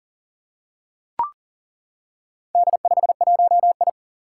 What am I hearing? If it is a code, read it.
Morse code, heard as a keyed tone, about 39 wpm, 700 Hz. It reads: D51I
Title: Four Challenging Letter-Number Combinations 40wpm